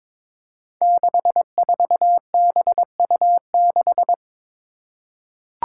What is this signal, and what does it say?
Morse code, heard as a keyed tone, about 22 wpm, 700 Hz. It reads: 64BU6
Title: Five Challenging Letter-Number Combinations 22wpm